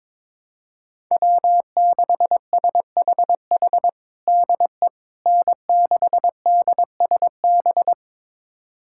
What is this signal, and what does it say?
Morse code, heard as a keyed tone, about 22 wpm, 700 Hz. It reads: W6SHH DE N6DSB
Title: USA Callsign DE Callsign - 5 Characters - 22wpm